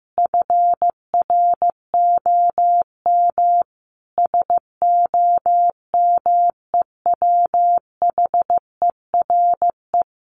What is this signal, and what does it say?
Morse code, heard as a keyed tone, about 15 wpm, 700 Hz. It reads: FROM SOMEWHERE
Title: Sentences from Top 1400 Words 15wpm